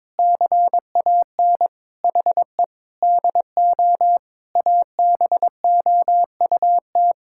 Morse code, heard as a keyed tone, about 22 wpm, 700 Hz. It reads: CAN HE DO ABOUT
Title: Sentences from Top 1000 Words - Review Part 6 - 22wpm